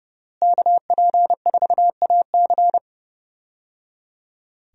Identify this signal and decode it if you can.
Morse code, heard as a keyed tone, about 30 wpm, 700 Hz. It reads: KP4AC
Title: USA Call Signs - 5 Characters - Repeated 3x - 30wpm